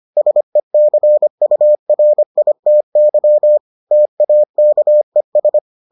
Morse code, heard as a keyed tone, about 25 wpm, 600 Hz. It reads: SECURITY TAKES